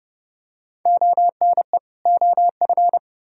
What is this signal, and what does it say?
Morse code, heard as a keyed tone, about 28 wpm, 700 Hz. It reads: ONE OF